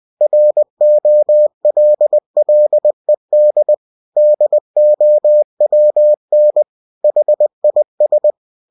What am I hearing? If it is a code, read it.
Morse code, heard as a keyed tone, about 20 wpm, 600 Hz. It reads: ROLLED DOWN HIS
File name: cw-books / winnie-the-pooh / winnie-20wpm-page-0019